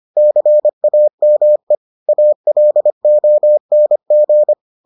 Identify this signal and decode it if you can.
Morse code, heard as a keyed tone, about 25 wpm, 600 Hz. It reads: CAME ALONG